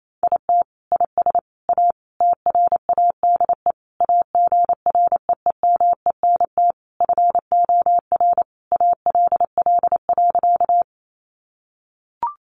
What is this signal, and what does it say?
Morse code, heard as a keyed tone, about 28 wpm, 700 Hz. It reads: IT IS A TRADE AGREEMENT FOR ALL.
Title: Sentences from Top 1500 Words 28wpm